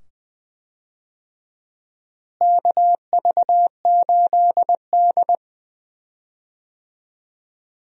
Morse code, heard as a keyed tone, about 20 wpm, 700 Hz. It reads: KV8D
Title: USA Call Signs - 4 Characters - Repeated 2x - 20wpm